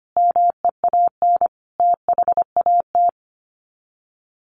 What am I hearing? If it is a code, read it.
Morse code, heard as a keyed tone, about 25 wpm, 700 Hz. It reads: MEAN THAT
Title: Sets of 5 Words Encore 25wpm